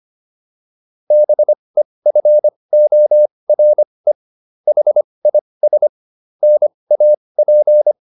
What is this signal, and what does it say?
Morse code, heard as a keyed tone, about 25 wpm, spaced as for 20 wpm, 600 Hz. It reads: BEFORE HIS NAP